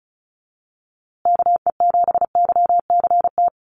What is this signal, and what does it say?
Morse code, heard as a keyed tone, about 34 wpm, 700 Hz. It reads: KE7YCT
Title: USA Call Signs Repeated 2x 35wpm